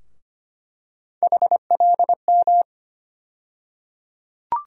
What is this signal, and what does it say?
Morse code, heard as a keyed tone, about 25 wpm, 700 Hz. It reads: HLM
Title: USA Call Sign Postfixes - 3 Letters 25wpm